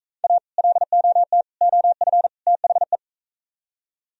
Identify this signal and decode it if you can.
Morse code, heard as a keyed tone, about 43 wpm, 700 Hz. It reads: A LOT OF THE